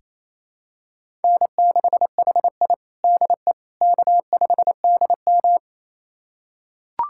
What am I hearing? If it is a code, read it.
Morse code, heard as a keyed tone, about 28 wpm, 700 Hz. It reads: N6HI DE K5DM
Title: USA Callsign DE Callsign - 4 Characters - 28wpm